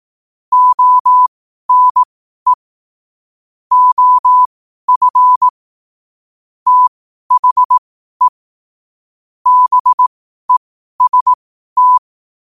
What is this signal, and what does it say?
Morse code, heard as a keyed tone, about 18 wpm, spaced as for 12 wpm, 1000 Hz. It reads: ONE OF THE BEST
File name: qotdfiles / 12wpm